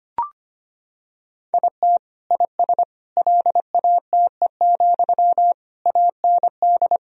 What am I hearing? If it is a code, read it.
Morse code, heard as a keyed tone, about 25 wpm, 700 Hz. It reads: IT IS LATE, AND
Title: Sentences from Top 1000 Words - Review Part 3 - 25wpm